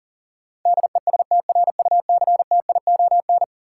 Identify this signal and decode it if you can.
Morse code, heard as a keyed tone, about 39 wpm, 700 Hz. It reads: DESTRUCTION
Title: Sets of 5 Words Encore - New Word Drill - 40wpm